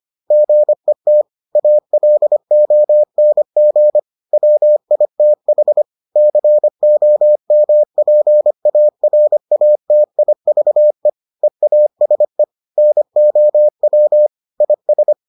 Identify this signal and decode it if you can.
Morse code, heard as a keyed tone, about 25 wpm, 600 Hz. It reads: GET ALONG WITH COMPARATIVE EASE NOW IS